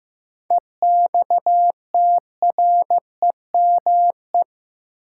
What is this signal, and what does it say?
Morse code, heard as a keyed tone, about 15 wpm, 700 Hz. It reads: EXTREME
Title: Sentences from Top 1300 Words - New Word Drill Alt - 15wpm